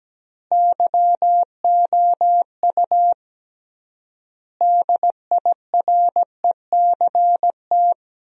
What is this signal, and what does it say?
Morse code, heard as a keyed tone, about 17 wpm, 700 Hz. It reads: YOU DIRECT
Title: 3x Word-Spacing - Sentences from Top 400 Words - 17wpm